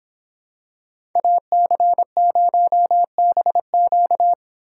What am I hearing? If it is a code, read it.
Morse code, heard as a keyed tone, about 25 wpm, 700 Hz. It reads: AC0BQ